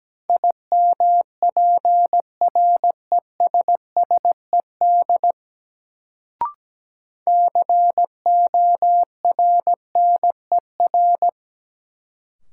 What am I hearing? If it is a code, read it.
Morse code, heard as a keyed tone, about 17 wpm, 700 Hz. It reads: IMPRESSED CORNER
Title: Sentences from Top 1600 Words - New Word Drill - 17wpm